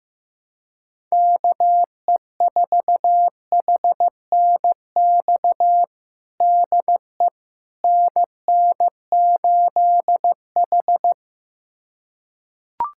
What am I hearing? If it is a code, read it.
Morse code, heard as a keyed tone, about 15 wpm, 700 Hz. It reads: KE4HNX DE NN8H
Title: USA Callsign DE Callsign 15wpm